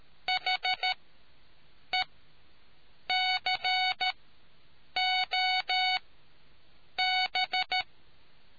Morse code, heard as a keyed tone, about 13 wpm, spaced as for 7 wpm, 740 Hz. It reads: HECOB